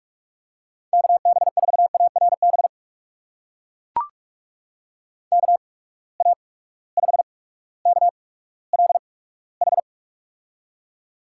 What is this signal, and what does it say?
Morse code, heard as a keyed tone, about 44 wpm, spaced as for 19 wpm, 700 Hz. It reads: KB4ARB KA5KLH
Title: USA Call Signs - 6 Characters - 8x Character Spacing - 45wpm